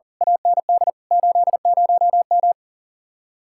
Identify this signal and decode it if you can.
Morse code, heard as a keyed tone, about 48 wpm, 700 Hz. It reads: AND 80M
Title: Speed-Racing Common QSO Phrases - 40wpm